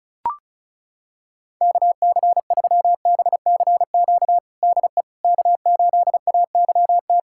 Morse code, heard as a keyed tone, about 36 wpm, 700 Hz. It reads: KC3BCQ DE K8AYT
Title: USA Callsign DE Callsign 35wpm